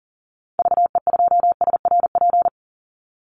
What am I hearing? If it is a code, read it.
Morse code, heard as a keyed tone, about 40 wpm, 700 Hz. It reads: VE2SRP
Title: Call Signs Repeated 3x - 40wpm